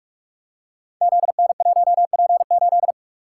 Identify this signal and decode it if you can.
Morse code, heard as a keyed tone, about 44 wpm, 700 Hz. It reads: ZN1P8
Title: Code Groups - 5 Letter:Numbers - 45wpm